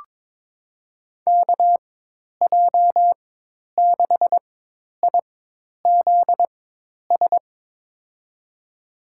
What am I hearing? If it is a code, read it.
Morse code, heard as a keyed tone, about 22 wpm, spaced as for 10 wpm, 700 Hz. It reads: KJ6IZS
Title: USA Call Signs - 6 Characters - 4x Character Spacing - 22wpm